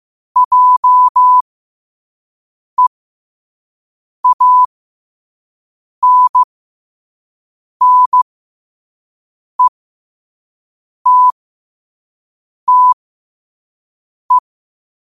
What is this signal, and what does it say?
Morse code, heard as a keyed tone, about 15 wpm, spaced as for 5 wpm, 1000 Hz. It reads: JEANNETTE
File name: 5wpm